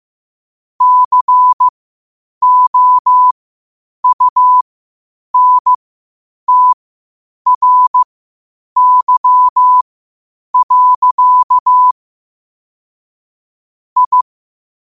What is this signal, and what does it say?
Morse code, heard as a keyed tone, about 15 wpm, spaced as for 8 wpm, 1000 Hz. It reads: COUNTRY. I
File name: qotdfiles / 8wpm